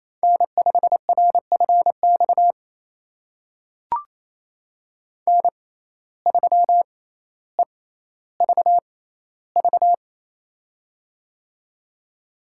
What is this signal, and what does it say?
Morse code, heard as a keyed tone, about 27 wpm, spaced as for 14 wpm, 700 Hz. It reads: N5RFX N3EVV
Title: USA Call Signs - 5 Characters - 6x Character Spacing - 28wpm